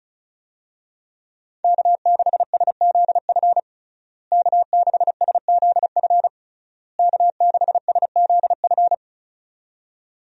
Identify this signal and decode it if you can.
Morse code, heard as a keyed tone, about 36 wpm, 700 Hz. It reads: K6SZF K6SZF K6SZF
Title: USA Call Signs - 5 Characters - Repeated 3x - 35wpm